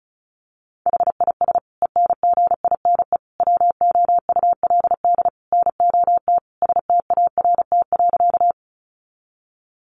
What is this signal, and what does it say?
Morse code, heard as a keyed tone, about 34 wpm, 700 Hz. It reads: HIS ENGINE WOULD NOT START.
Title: Sentences from Top 1500 Words 35wpm